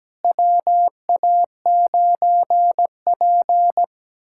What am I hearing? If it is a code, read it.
Morse code, heard as a keyed tone, about 17 wpm, 700 Hz. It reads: WA9P
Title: USA Callsign DE Callsign - 4 Characters - 17wpm